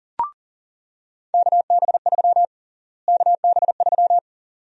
Morse code, heard as a keyed tone, about 40 wpm, 700 Hz. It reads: KB3 KB3
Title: USA Call Sign Prefixes - Repeated 2x - 40wpm